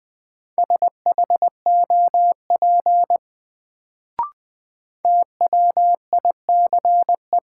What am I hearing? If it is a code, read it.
Morse code, heard as a keyed tone, about 20 wpm, 700 Hz. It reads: SHOP TWICE
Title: Sentences from Top 1600 Words - New Word Drill - 20wpm